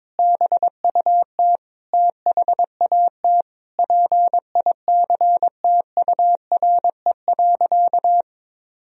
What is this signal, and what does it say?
Morse code, heard as a keyed tone, about 22 wpm, 700 Hz. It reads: BUT THAT PICTURE.
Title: Sentences from Top 1000 Words - Review Part 1 - 22wpm